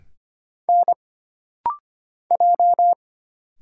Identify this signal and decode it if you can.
Morse code, heard as a keyed tone, about 25 wpm, 700 Hz. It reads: N J
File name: Single Character (Letter:Number:Punctuation:Prosign) - Rapid-Fire - 25wpm